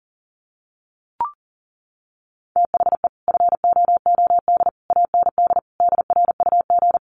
Morse code, heard as a keyed tone, about 40 wpm, 700 Hz. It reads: THE FOOD AND DRUG
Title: Sets of 5 Words 40wpm